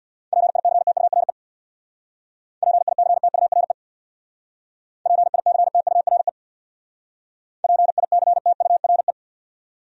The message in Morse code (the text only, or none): PICTURE PICTURE PICTURE PICTURE